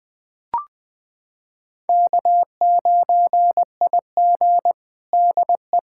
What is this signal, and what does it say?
Morse code, heard as a keyed tone, about 21 wpm, 700 Hz. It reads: K9IG DE
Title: USA Callsign DE Callsign - 4 Characters - 20wpm